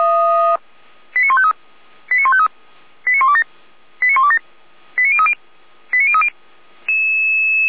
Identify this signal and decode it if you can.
Five-tone selective call (ZVEI), heard as quick runs of five tones, tones of about 70 ms. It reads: 89153 89153 89127 89127 89030 89030